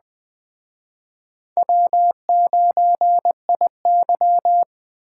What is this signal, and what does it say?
Morse code, heard as a keyed tone, about 21 wpm, 700 Hz. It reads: W9IY
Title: Speed Racing - USA 1x2 Call Signs 17wpm